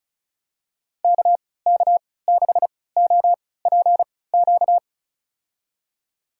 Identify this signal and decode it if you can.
Morse code, heard as a keyed tone, about 36 wpm, spaced as for 20 wpm, 700 Hz. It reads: KK6OPQ
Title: USA Call Signs - 6 Characters - 3x Character Spacing - 35wpm